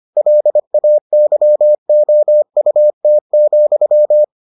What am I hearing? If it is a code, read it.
Morse code, heard as a keyed tone, about 25 wpm, 600 Hz. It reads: LAYOUT,